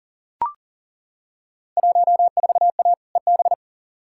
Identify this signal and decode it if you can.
Morse code, heard as a keyed tone, about 40 wpm, 700 Hz. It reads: 14A EB
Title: Field Day Exchanges Encore - 40wpm